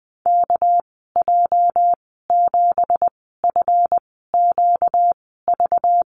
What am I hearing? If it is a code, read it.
Morse code, heard as a keyed tone, about 20 wpm, spaced as for 14 wpm, 700 Hz. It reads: KJ7FQV
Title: 238 Lesson 38g - Letter q - Call Signs 20wpm